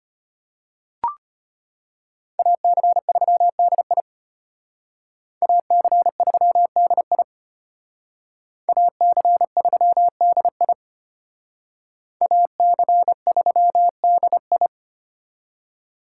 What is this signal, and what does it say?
Morse code, heard as a keyed tone, about 32 wpm, 700 Hz. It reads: AC3DI AC3DI AC3DI AC3DI